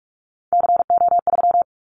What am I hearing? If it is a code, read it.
Morse code, heard as a keyed tone, about 44 wpm, 700 Hz. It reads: CO3
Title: Callsign Prefixes for Top 100 DXCC - 45wpm